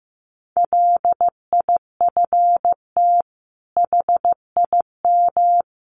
Morse code, heard as a keyed tone, about 15 wpm, 700 Hz. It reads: LIFT HIM
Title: Sentences from Top 1000 Words 15wpm